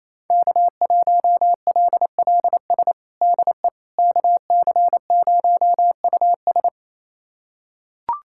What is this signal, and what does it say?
Morse code, heard as a keyed tone, about 28 wpm, 700 Hz. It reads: K1LLS DE KC0US
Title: USA Callsign DE Callsign - 5 Characters - 28wpm